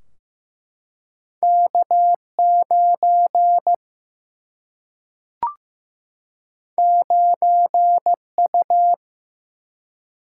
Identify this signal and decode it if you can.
Morse code, heard as a keyed tone, about 15 wpm, 700 Hz. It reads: K 9 9 U
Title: Single-Letter Single-Number 15wpm